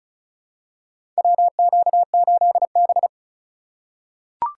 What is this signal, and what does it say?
Morse code, heard as a keyed tone, about 36 wpm, 700 Hz. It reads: WQ8B
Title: USA Call Signs - 4x Character Spacing - 35wpm